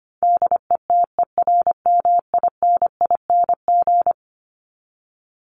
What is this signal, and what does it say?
Morse code, heard as a keyed tone, about 25 wpm, 700 Hz. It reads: DETERMINING
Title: ING Suffix - Top 100 Words with Base Words 25wpm